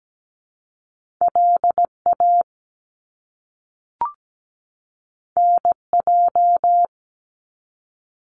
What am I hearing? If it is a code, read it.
Morse code, heard as a keyed tone, about 17 wpm, 700 Hz. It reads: LA NJ
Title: US States 17wpm